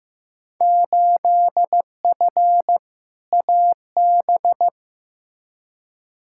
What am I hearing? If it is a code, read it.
Morse code, heard as a keyed tone, about 15 wpm, 700 Hz. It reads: 8F AB
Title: Field Day Exchanges Encore - 15wpm